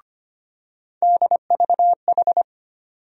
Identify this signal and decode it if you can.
Morse code, heard as a keyed tone, about 25 wpm, 700 Hz. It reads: DVH
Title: USA Call Sign Postfixes - 3 Letters 25wpm